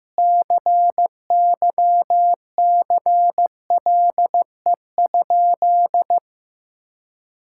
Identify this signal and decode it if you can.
Morse code, heard as a keyed tone, about 16 wpm, 700 Hz. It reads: CYCLE?
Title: Sentences from Top 1000 Words - Review Part 6 - 15wpm